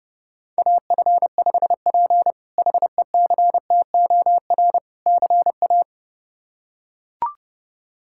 Morse code, heard as a keyed tone, about 30 wpm, 700 Hz. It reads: AF5P HECTOR CA
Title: K1USN SST QSO Practice - 30wpm